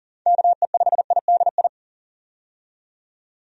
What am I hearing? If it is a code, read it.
Morse code, heard as a keyed tone, about 41 wpm, 700 Hz. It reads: KE5IDI
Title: USA Call Signs - 6 Characters - 40wpm